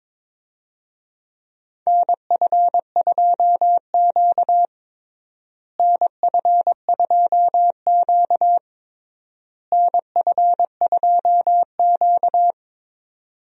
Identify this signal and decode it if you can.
Morse code, heard as a keyed tone, about 22 wpm, 700 Hz. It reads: NF2Q NF2Q NF2Q
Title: USA Call Signs - 4 Characters - Repeated 3x - 22wpm